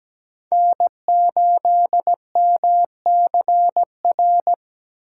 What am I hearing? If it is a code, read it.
Morse code, heard as a keyed tone, about 17 wpm, 700 Hz. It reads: N8MCR
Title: USA Call Signs - 5 Characters - 8x Character Spacing - 17wpm